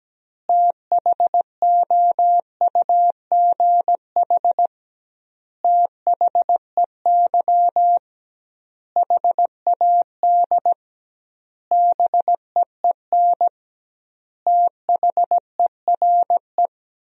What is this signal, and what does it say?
Morse code, heard as a keyed tone, about 17 wpm, 700 Hz. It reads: THOUGH THEY HAD BEEN THERE
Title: 2x Word-Spacing - Sentences from Top 400 Words - 17wpm